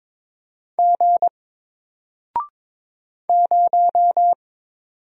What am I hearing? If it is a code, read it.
Morse code, heard as a keyed tone, about 22 wpm, 700 Hz. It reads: G 0
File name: Single-Letter-Number-22wpm